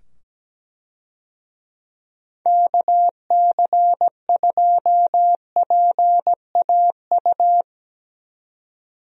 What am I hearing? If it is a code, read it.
Morse code, heard as a keyed tone, about 17 wpm, 700 Hz. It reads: KC2PAU
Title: USA Call Signs - 6 Characters - 17wpm